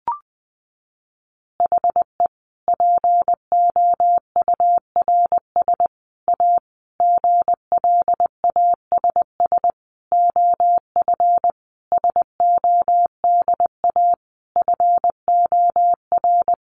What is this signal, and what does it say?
Morse code, heard as a keyed tone, about 19 wpm, 700 Hz. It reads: HE POURS A GLASS OF SODA FOR
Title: Additional Easy Phrases 20wpm